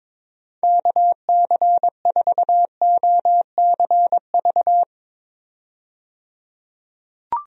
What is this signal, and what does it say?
Morse code, heard as a keyed tone, about 22 wpm, 700 Hz. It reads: KC4OCV
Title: USA Call Signs - 6 Characters - 22wpm